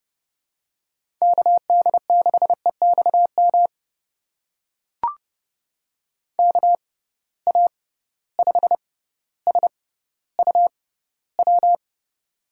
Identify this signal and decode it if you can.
Morse code, heard as a keyed tone, about 30 wpm, spaced as for 15 wpm, 700 Hz. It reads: KD6EXM KA5SUW
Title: USA Call Signs - 6 Characters - 6x Character Spacing - 30wpm